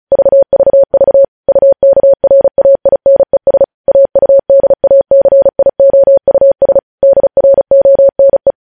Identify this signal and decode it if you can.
Morse code, heard as a keyed tone, about 36 wpm, 550 Hz. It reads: VVV UKRAINES AUDACIOUS DRONE
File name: podcast / News headlines-1x-35x35